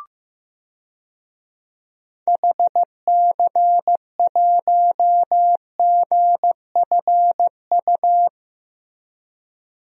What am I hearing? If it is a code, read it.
Morse code, heard as a keyed tone, about 14 wpm, 700 Hz. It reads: HC1GFU
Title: DXCC-100 Call Signs - 6 Characters - Repeated 3x - 15wpm